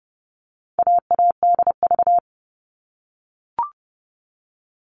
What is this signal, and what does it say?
Morse code, heard as a keyed tone, about 30 wpm, 700 Hz. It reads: AADV
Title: All Cut Numbers - 4-Digit 30wpm